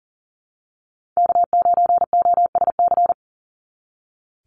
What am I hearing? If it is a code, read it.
Morse code, heard as a keyed tone, about 40 wpm, 700 Hz. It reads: K9OSC
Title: NA SOTA Chaser Callsigns 40wpm